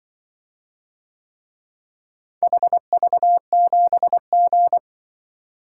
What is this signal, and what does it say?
Morse code, heard as a keyed tone, about 24 wpm, 700 Hz. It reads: HV7G
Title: Speed-Racing - Four Letter-Number Combinations - 20wpm